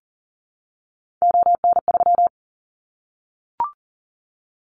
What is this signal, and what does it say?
Morse code, heard as a keyed tone, about 39 wpm, 700 Hz. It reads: ON3
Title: Callsign Prefixes for Top 100 DXCC - 40wpm